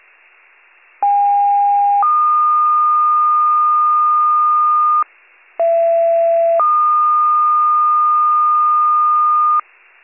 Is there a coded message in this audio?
Yes: two-tone page from 1.0 to 9.6 s, heard as a one-second tone, then a three-second tone, about 810 then 1220 Hz, about 670 then 1220 Hz.